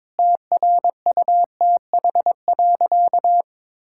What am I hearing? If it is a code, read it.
Morse code, heard as a keyed tone, about 22 wpm, 700 Hz. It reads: TRUTH.
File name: Sentences with 9-Word Phrases - 22wpm